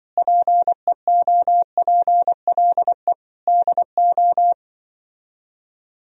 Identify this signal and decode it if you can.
Morse code, heard as a keyed tone, about 25 wpm, 700 Hz. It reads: PEOPLE DO